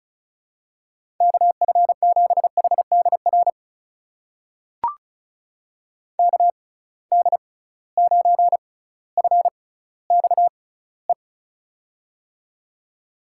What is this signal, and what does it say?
Morse code, heard as a keyed tone, about 34 wpm, spaced as for 18 wpm, 700 Hz. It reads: KF7HDR KD9FXE
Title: USA Call Signs - 6 Characters - 6x Character Spacing - 35wpm